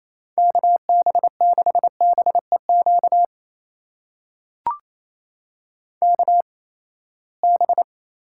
Morse code, heard as a keyed tone, about 28 wpm, spaced as for 19 wpm, 700 Hz. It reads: KB6BEQ KB
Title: USA Call Signs - 6 Characters - 8x Character Spacing - 28wpm